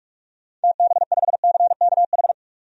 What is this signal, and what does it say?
Morse code, heard as a keyed tone, about 42 wpm, 700 Hz. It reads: TB5CKH
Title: DXCC-100 Call Signs - 6 Characters - 45wpm